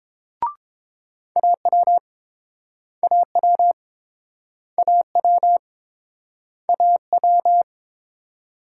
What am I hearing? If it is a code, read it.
Morse code, heard as a keyed tone, about 30 wpm, 700 Hz. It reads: AW AW AW AW